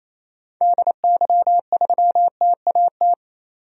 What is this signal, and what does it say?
Morse code, heard as a keyed tone, about 28 wpm, 700 Hz. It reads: DY3TAT
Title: DXCC-100 Call Signs - 6 Characters - 28wpm